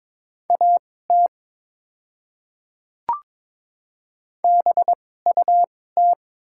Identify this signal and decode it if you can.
Morse code, heard as a keyed tone, about 22 wpm, spaced as for 16 wpm, 690 Hz. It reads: AT BUT
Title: 2x Character Spacing - Top 300 Words - 22wpm